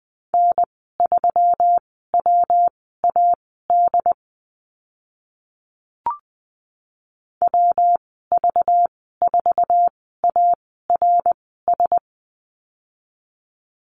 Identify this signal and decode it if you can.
Morse code, heard as a keyed tone, about 20 wpm, spaced as for 15 wpm, 700 Hz. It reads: N3WAD WV4ARS
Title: 245 Lesson 39g - Letter z - Call Signs 20wpm